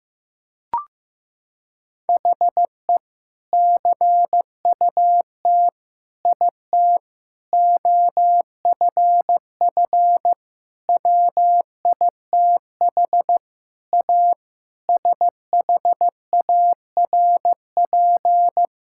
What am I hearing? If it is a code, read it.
Morse code, heard as a keyed tone, about 15 wpm, 700 Hz. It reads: HE CUT IT OFF WITH A SHARP
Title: Sentences from Top 1600 Words 15wpm